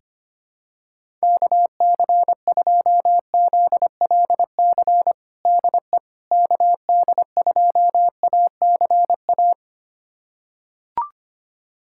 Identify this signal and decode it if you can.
Morse code, heard as a keyed tone, about 25 wpm, 700 Hz. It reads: KC2ZLC DE KD2ACA
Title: USA Callsign DE Callsign 25wpm